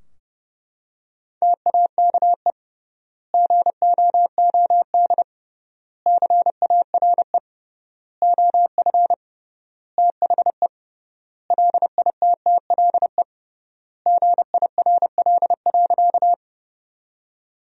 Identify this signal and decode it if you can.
Morse code, heard as a keyed tone, about 30 wpm, 700 Hz. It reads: TAKE GOOD CARE OF THE LITTLE GIRL.
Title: 3x Word-Spacing - Sentences from Top 400 Words - 30wpm